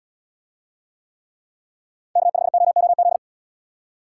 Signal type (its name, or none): Morse code